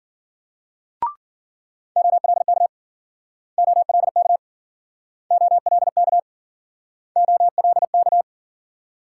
Morse code, heard as a keyed tone, about 55 wpm, 700 Hz. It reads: OLK OLK OLK OLK